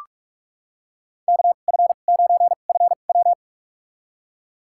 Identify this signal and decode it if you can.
Morse code, heard as a keyed tone, about 46 wpm, spaced as for 33 wpm, 700 Hz. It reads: KF9FW
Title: USA Call Signs - 2x Character Spacing - 45wpm